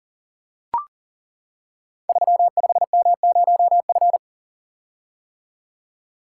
Morse code, heard as a keyed tone, about 40 wpm, 700 Hz. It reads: 35M0F
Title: Code Groups - 5 Letter:Numbers - 40wpm